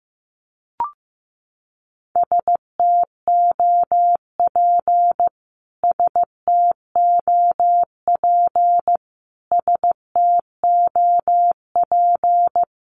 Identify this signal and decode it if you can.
Morse code, heard as a keyed tone, about 15 wpm, 700 Hz. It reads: STOP STOP STOP
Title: Top 300 Words Repeated 15wpm